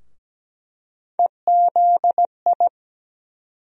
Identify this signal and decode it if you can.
Morse code, heard as a keyed tone, about 17 wpm, 700 Hz. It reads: EZI
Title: USA Call Sign Postfixes - 3 Letters 17wpm